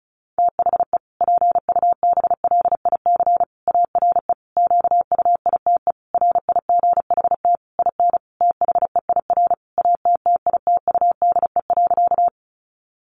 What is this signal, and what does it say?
Morse code, heard as a keyed tone, about 34 wpm, 700 Hz. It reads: THE PUBLIC ARE QUITE RIGHT IN THEIR ATTITUDE.
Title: Sentences from Top 1300 Words 35wpm